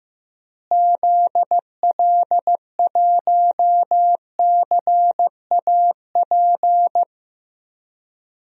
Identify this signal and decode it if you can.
Morse code, heard as a keyed tone, about 15 wpm, 700 Hz. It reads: ZL1CAP
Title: DXCC-100 Call Signs - 6 Characters - Repeated 2x - 15wpm